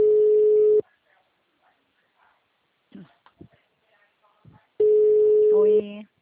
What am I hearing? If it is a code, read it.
Ringback tone (Europe, 425 Hz), with speech.